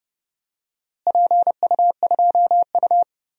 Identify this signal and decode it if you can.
Morse code, heard as a keyed tone, about 32 wpm, 700 Hz. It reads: PU2U
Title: DXCC-100 Call Signs - 4 Characters - Repeated 3x - 30wpm